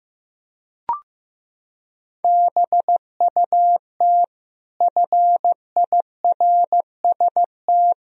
Morse code, heard as a keyed tone, about 15 wpm, 700 Hz. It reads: BUT FIRST
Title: Sentences from Top 1000 Words - Review Part 2 - 15wpm